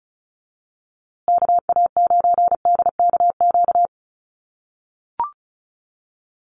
Morse code, heard as a keyed tone, about 34 wpm, 700 Hz. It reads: KA9DKQ
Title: USA Call Signs Repeated 3x 35wpm